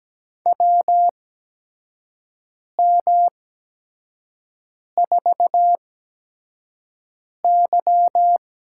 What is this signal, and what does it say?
Morse code, heard as a keyed tone, about 17 wpm, spaced as for 5 wpm, 700 Hz. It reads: WM4Y
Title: USA Call Signs - 8x Character Spacing - 17wpm